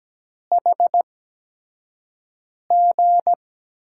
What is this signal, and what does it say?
Morse code, heard as a keyed tone, about 17 wpm, spaced as for 5 wpm, 700 Hz. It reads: HG
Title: USA Call Signs - 5 Characters - 8x Character Spacing - 17wpm